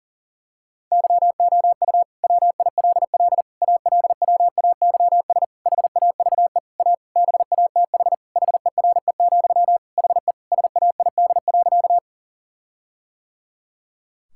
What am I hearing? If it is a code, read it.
Morse code, heard as a keyed tone, about 40 wpm, 700 Hz. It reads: YOU WILL ALWAYS HAVE A BATH HERE, HE SAID.
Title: Sentences from Top 1600 Words 40wpm